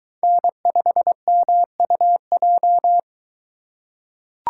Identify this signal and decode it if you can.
Morse code, heard as a keyed tone, about 21 wpm, 700 Hz. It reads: N5MUJ